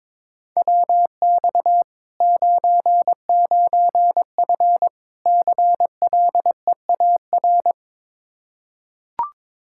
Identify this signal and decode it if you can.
Morse code, heard as a keyed tone, about 21 wpm, 700 Hz. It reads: WX 99F CLEAR
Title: Practice QSO - Exchange 2 - Info - 22wpm